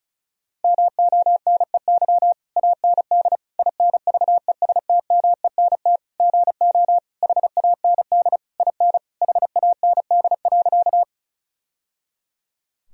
Morse code, heard as a keyed tone, about 34 wpm, 700 Hz. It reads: MONEY AND INVESTMENT GO HAND IN HAND.
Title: Sentences from Top 1000 Words - Review Part 4 - 35wpm